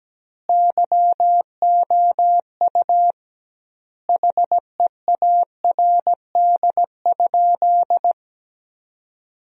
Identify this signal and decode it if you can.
Morse code, heard as a keyed tone, about 17 wpm, 700 Hz. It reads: YOU HEARD?
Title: 2x Word-Spacing - Sentences from Top 400 Words - 17wpm